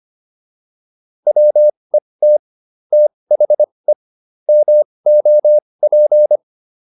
Morse code, heard as a keyed tone, about 25 wpm, spaced as for 20 wpm, 600 Hz. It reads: WET THE MOP